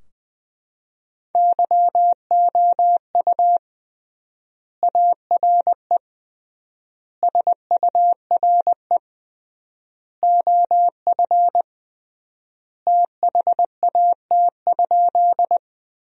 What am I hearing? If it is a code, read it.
Morse code, heard as a keyed tone, about 20 wpm, 700 Hz. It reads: YOU ARE SURE OF THAT?
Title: 3x Word-Spacing - Sentences from Top 300 Words - 20wpm